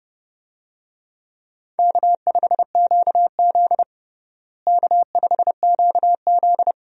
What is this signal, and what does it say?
Morse code, heard as a keyed tone, about 30 wpm, 700 Hz. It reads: K5QZ K5QZ
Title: USA Call Signs - 4 Characters - Repeated 2x - 30wpm